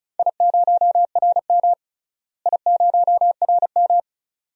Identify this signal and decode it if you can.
Morse code, heard as a keyed tone, about 34 wpm, 700 Hz. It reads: I0RM I0RM
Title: DXCC-100 Call Signs - 4 Characters - Repeated 2x - 35wpm